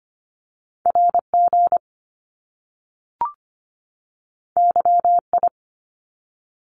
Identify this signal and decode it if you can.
Morse code, heard as a keyed tone, about 25 wpm, 700 Hz. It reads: RG YI
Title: Top Two Letter Combinations in Words 25wpm